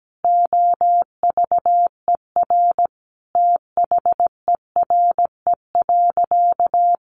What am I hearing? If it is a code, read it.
Morse code, heard as a keyed tone, about 17 wpm, 700 Hz. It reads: OVER THERE.
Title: Sentences from Top 1000 Words 17wpm